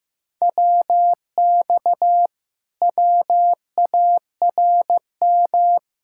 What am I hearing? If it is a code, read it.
Morse code, heard as a keyed tone, about 15 wpm, 700 Hz. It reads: WX WARM
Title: Speed-Racing Common QSO Phrases Encore - 15wpm